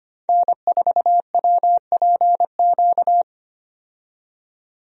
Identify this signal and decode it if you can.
Morse code, heard as a keyed tone, about 25 wpm, 700 Hz. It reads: N4WPQ
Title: USA Call Signs - 5 Characters - 25wpm